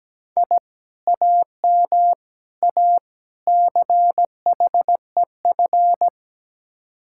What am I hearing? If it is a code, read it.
Morse code, heard as a keyed tone, about 19 wpm, 700 Hz. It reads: I AM A CHEF
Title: Speed-Racing Common QSO Phrases Encore - 17wpm